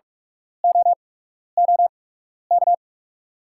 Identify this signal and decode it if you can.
Morse code, heard as a keyed tone, about 44 wpm, spaced as for 12 wpm, 690 Hz. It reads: OOK
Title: 8x Character Spacing - Top 200 Words - 45wpm